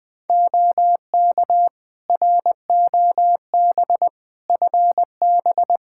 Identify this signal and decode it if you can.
Morse code, heard as a keyed tone, about 20 wpm, 700 Hz. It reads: OK ROB FB
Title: Practice QSO - Exchange 2 - Info - 20wpm